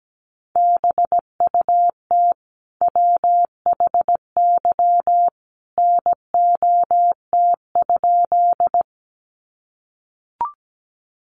Morse code, heard as a keyed tone, about 17 wpm, 700 Hz. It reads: BUT WHY NOT?
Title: Sentences from Top 600 Words 17wpm